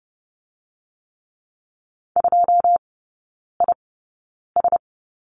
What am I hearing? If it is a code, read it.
Morse code, heard as a keyed tone, about 30 wpm, spaced as for 9 wpm, 700 Hz. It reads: 2IS
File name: Lesson 16e - Letter c - Three Characters 30wpm